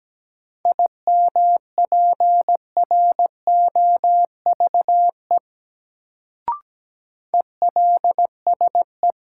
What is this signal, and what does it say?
Morse code, heard as a keyed tone, about 17 wpm, 700 Hz. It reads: IMPROVE ELSE
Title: Sentences from Top 800 Words - New Word Drill - 17wpm